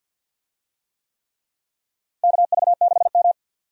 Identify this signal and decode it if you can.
Morse code, heard as a keyed tone, about 50 wpm, 700 Hz. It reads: KV6M
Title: USA Call Signs - 4 Characters - Repeated 3x - 50wpm